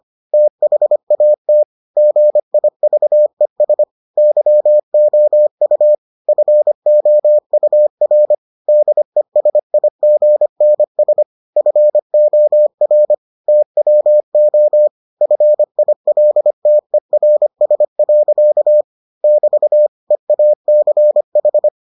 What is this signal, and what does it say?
Morse code, heard as a keyed tone, about 25 wpm, 600 Hz. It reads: THAT GIVES YOU FOUR DESIGNS FOR TWO FILTERS. <BT> EACH